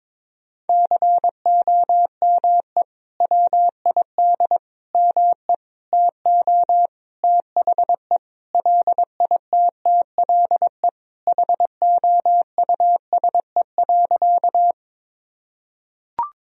Morse code, heard as a keyed tone, about 22 wpm, 700 Hz. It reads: COME WID ME TO THE LITTLE HOUSE.
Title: Sentences from Top 200 Words and Abbreviations 22wpm